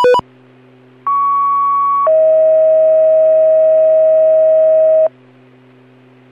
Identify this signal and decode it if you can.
Two-tone page, heard as a one-second tone, then a three-second tone, about 1120 then 640 Hz. It is heard from 1.1 to 5.1 s.